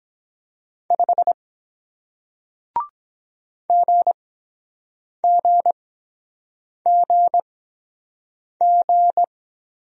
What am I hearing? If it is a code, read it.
Morse code, heard as a keyed tone, about 23 wpm, spaced as for 7 wpm, 700 Hz. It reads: 5 GGGG